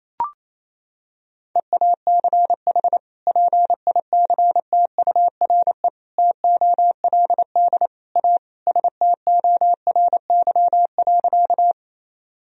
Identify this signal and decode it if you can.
Morse code, heard as a keyed tone, about 28 wpm, 700 Hz. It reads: EACH PICTURE TOLD A STORY.
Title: Sentences from Top 1000 Words - Review Part 1 - 28wpm